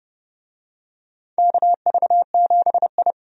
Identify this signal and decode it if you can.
Morse code, heard as a keyed tone, about 30 wpm, 700 Hz. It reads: KV7I
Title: USA Call Signs - 4 Characters - 30wpm